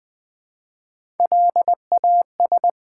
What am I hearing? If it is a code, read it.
Morse code, heard as a keyed tone, about 20 wpm, 700 Hz. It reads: LAS
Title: Build A Word - 1 Letter At A Time - 3 to 5-Letter Words - 20wpm